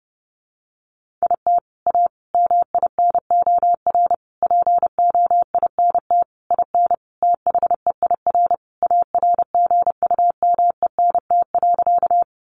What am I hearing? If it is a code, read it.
Morse code, heard as a keyed tone, about 30 wpm, 700 Hz. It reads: IT A MINOR POINT IN THEIR ARGUMENT.
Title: Sentences from Top 1500 Words 30wpm